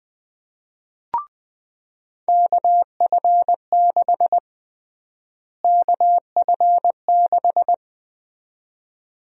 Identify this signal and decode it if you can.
Morse code, heard as a keyed tone, about 20 wpm, 700 Hz. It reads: KF6 KF6
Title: USA Call Sign Prefixes - Repeated 2x - 20wpm